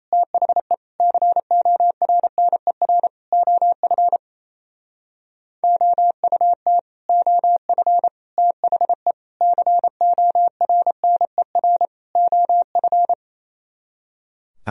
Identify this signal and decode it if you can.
Morse code, heard as a keyed tone, about 28 wpm, 700 Hz. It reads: THE CORNER OF OUT OF THE CORNER OF
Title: Speed-Racing - Sets of 5 Words 28wpm